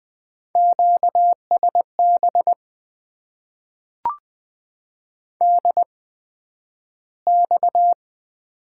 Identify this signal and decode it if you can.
Morse code, heard as a keyed tone, about 20 wpm, spaced as for 11 wpm, 700 Hz. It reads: QSB DX
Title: Top 90 QSO Elements - 8x Character Spacing - 20wpm